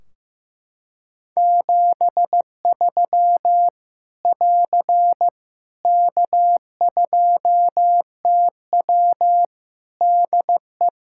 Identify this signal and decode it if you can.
Morse code, heard as a keyed tone, about 14 wpm, 700 Hz. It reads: 73 <AR> K2TW DE
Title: Practice QSO - Exchange 4 - Ending - 15wpm